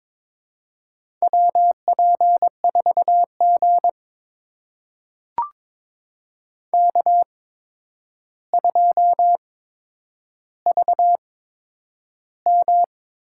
Morse code, heard as a keyed tone, about 22 wpm, spaced as for 9 wpm, 700 Hz. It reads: WP4G K2VM
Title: USA Call Signs - 4 Characters - 8x Character Spacing - 22wpm